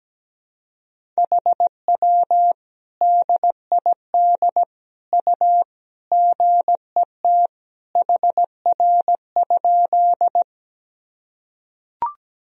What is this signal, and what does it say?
Morse code, heard as a keyed tone, about 17 wpm, 700 Hz. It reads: HW DID U GET HR?
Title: Sentences from Top 200 Words and Abbreviations 17wpm